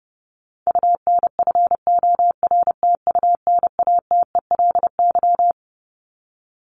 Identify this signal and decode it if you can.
Morse code, heard as a keyed tone, about 30 wpm, 700 Hz. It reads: UNFORTUNATELY
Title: Top 1751-2000 Words 30wpm